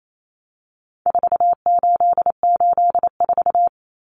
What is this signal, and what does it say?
Morse code, heard as a keyed tone, about 28 wpm, 700 Hz. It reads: 4884
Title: Common Cut Numbers - 4-Digit 28wpm